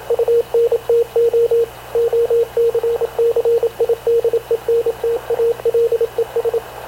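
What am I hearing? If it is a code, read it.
Morse code, heard as a keyed tone, about 27 wpm, 460 Hz, with radio noise. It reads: UNTO OCCIDENTALES
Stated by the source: Havana Radio CLA press (PX) broadcast, recorded by Radio Officer Paul Dumesnil aboard mv CNAV QUEST/CZDO.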